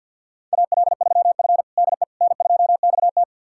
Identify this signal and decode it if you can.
Morse code, heard as a keyed tone, about 52 wpm, 700 Hz. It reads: AL3F DE N2XT
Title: USA Callsign DE Callsign - 4 Characters - 50wpm